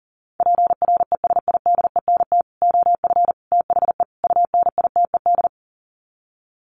Morse code, heard as a keyed tone, about 40 wpm, 700 Hz. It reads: PRESIDENT OF THE UNITED
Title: Sets of 4 Words 40wpm